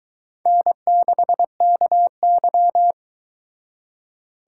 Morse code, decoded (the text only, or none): N6KY